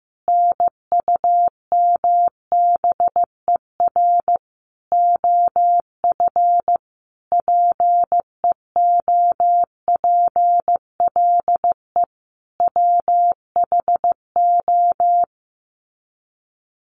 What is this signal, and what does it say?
Morse code, heard as a keyed tone, about 15 wpm, 700 Hz. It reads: NUMBER OF PEOPLE WHO
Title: Sets of 5 Words Encore 15wpm